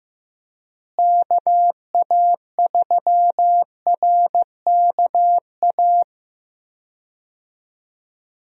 Morse code, heard as a keyed tone, about 15 wpm, 700 Hz. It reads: KA3RKA
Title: USA Call Signs - 6 Characters - Repeated 2x - 15wpm